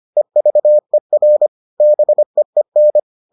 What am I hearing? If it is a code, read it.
Morse code, heard as a keyed tone, about 25 wpm, 600 Hz. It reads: EVER BEEN